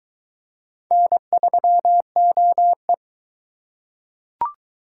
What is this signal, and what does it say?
Morse code, heard as a keyed tone, about 23 wpm, 700 Hz. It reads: N3OE